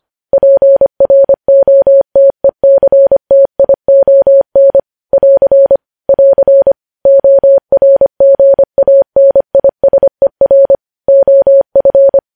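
Morse code, heard as a keyed tone, about 25 wpm, 550 Hz. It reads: PROTECTION <AR> <AR> ORGANISER OF
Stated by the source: Morse code news headlines sent at 25 wpm.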